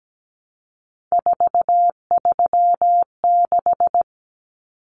Morse code, heard as a keyed tone, about 17 wpm, 700 Hz. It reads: 436
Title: Numbers - 3-Digit 17wpm